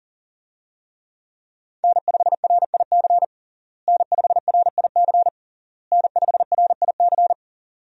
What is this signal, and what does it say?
Morse code, heard as a keyed tone, about 40 wpm, 700 Hz. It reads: N5RIC N5RIC N5RIC
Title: USA Call Signs - 5 Characters - Repeated 3x - 40wpm